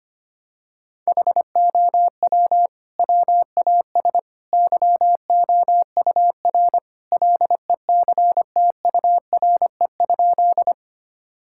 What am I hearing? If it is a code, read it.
Morse code, heard as a keyed tone, about 25 wpm, 700 Hz. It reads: HOW WAS YOUR LECTURE?
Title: Sentences from Top 1600 Words 25wpm